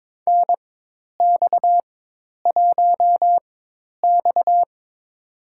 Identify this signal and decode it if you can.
Morse code, heard as a keyed tone, about 22 wpm, spaced as for 10 wpm, 700 Hz. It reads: NX1X
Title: USA Call Signs - 4x Character Spacing - 22wpm